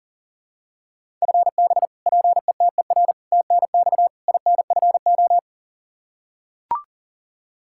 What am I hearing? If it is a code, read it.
Morse code, heard as a keyed tone, about 40 wpm, 700 Hz. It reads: FB PETER TNX INFO
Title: Practice QSO - Exchange 3 - More Info - 40wpm